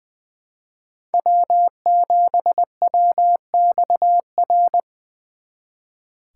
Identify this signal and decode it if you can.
Morse code, heard as a keyed tone, about 24 wpm, 700 Hz. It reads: W7WXR
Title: Speed Racing - USA 1x3 Call Signs 20wpm